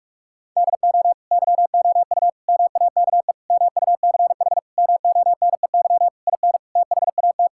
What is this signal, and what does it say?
Morse code, heard as a keyed tone, about 41 wpm, 700 Hz. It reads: DO YOU MAKE MUCH MONEY IN THAT
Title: Sentences from Top 1000 Words - Review Part 6 - 45wpm